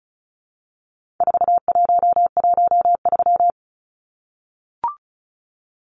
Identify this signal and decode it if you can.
Morse code, heard as a keyed tone, about 34 wpm, 700 Hz. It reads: 4113
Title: Numbers - 4-Digit 35wpm